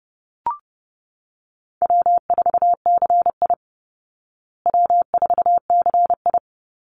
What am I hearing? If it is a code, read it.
Morse code, heard as a keyed tone, about 30 wpm, 700 Hz. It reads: W4CI W4CI
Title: Call Signs Repeated 2x - 30wpm